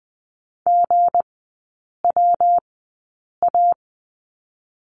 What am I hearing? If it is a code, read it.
Morse code, heard as a keyed tone, about 20 wpm, spaced as for 8 wpm, 690 Hz. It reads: GWA
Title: 159 Lesson 26e - Letter g - Three Characters 20wpm